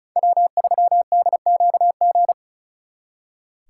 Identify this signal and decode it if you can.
Morse code, heard as a keyed tone, about 43 wpm, 700 Hz. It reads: W3DQG
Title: Speed Racing - USA 1x3 Call Signs 35wpm